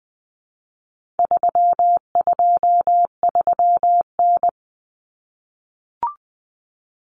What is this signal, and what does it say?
Morse code, heard as a keyed tone, about 20 wpm, 700 Hz. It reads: 323N
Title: Common Cut Numbers - 4-Digit 20wpm